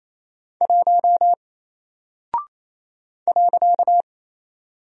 Morse code, heard as a keyed tone, about 27 wpm, 700 Hz. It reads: 1 .
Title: Single Character (Letter:Number:Punctuation:Prosign) Alt 28wpm